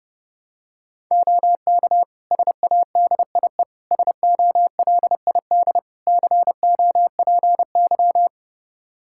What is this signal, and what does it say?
Morse code, heard as a keyed tone, about 30 wpm, 700 Hz. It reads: OK SADIE SOLID COPY
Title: Practice QSO - Exchange 3 - More Info - 30wpm